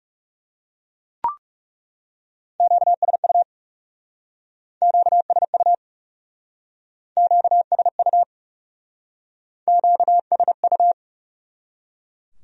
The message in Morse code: QSU QSU QSU QSU

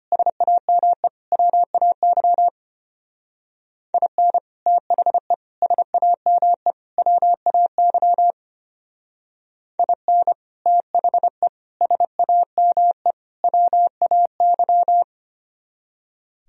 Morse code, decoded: SAME WAY IN THE SAME WAY IN THE SAME WAY

